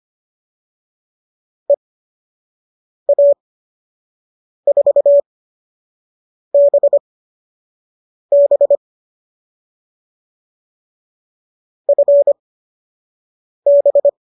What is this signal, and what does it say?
Morse code, heard as a keyed tone, about 25 wpm, spaced as for 6 wpm, 570 Hz. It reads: EA4BB FB